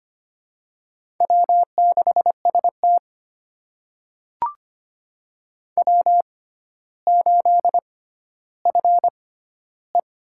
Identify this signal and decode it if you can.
Morse code, heard as a keyed tone, about 25 wpm, spaced as for 13 wpm, 700 Hz. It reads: W6ST W8FE
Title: USA Call Signs - 4 Characters - 6x Character Spacing - 25wpm